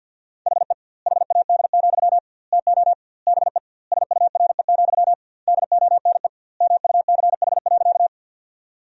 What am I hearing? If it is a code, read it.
Morse code, heard as a keyed tone, about 50 wpm, 700 Hz. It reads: HE HAD, TO BE SURE, DONE MUCH.
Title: 2x Word-Spacing - Sentences from Top 500 Words - 50wpm